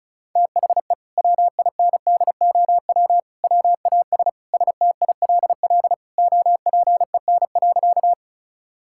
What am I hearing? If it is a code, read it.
Morse code, heard as a keyed tone, about 36 wpm, 700 Hz. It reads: THE WINDOW WAS STILL OPEN.
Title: Sentences from Top 1600 Words 35wpm